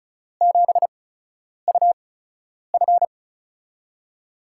Morse code, heard as a keyed tone, about 34 wpm, spaced as for 10 wpm, 700 Hz. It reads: 7UF
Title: USA Call Signs - 5 Characters - 8x Character Spacing - 35wpm